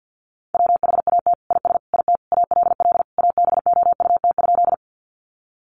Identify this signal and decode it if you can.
Morse code, heard as a keyed tone, about 50 wpm, 700 Hz. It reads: WHAT IS IT ALL ABOUT?
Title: Sentences from Top 100 Words 50wpm